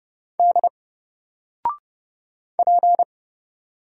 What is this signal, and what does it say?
Morse code, heard as a keyed tone, about 30 wpm, 700 Hz. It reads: D P
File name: Single-Letter-Number-30wpm